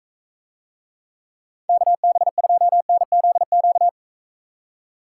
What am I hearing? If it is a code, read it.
Morse code, heard as a keyed tone, about 44 wpm, 700 Hz. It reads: KB2NZQ